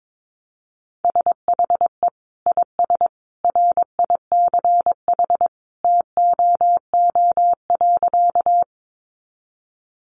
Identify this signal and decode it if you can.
Morse code, heard as a keyed tone, about 22 wpm, 700 Hz. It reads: SHE IS RICH TOO.
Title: Sentences from Top 1100 Words 22wpm